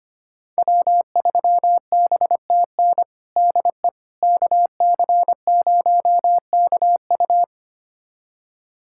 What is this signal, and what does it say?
Morse code, heard as a keyed tone, about 25 wpm, 700 Hz. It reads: W3BTN DE KC0KU
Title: USA Callsign DE Callsign - 5 Characters - 25wpm